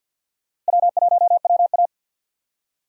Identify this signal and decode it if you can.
Morse code, heard as a keyed tone, about 48 wpm, 700 Hz. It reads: W1WA
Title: USA Call Signs - 4 Characters - 6x Character Spacing - 50wpm